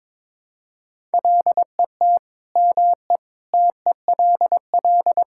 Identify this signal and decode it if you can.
Morse code, heard as a keyed tone, about 21 wpm, 700 Hz. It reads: LET ME TELL
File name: practice / 22wpm / Sentences with 4-Word Phrases - 22wpm